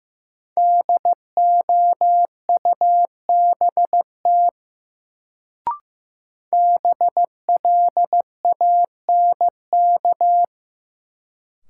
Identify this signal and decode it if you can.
Morse code, heard as a keyed tone, about 16 wpm, 700 Hz. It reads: DOUBT BLANK
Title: Sentences from Top 1500 Words - New Word Drill - 15wpm